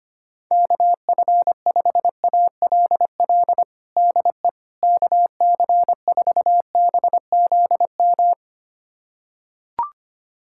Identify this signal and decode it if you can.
Morse code, heard as a keyed tone, about 25 wpm, 700 Hz. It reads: KF5ALL DE KC4BZM
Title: USA Callsign DE Callsign - 6 Characters - 25wpm